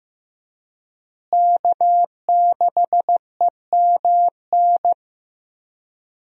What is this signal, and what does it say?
Morse code, heard as a keyed tone, about 15 wpm, 700 Hz. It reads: K6EMN
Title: USA Call Signs - 5 Characters - Repeated 2x - 15wpm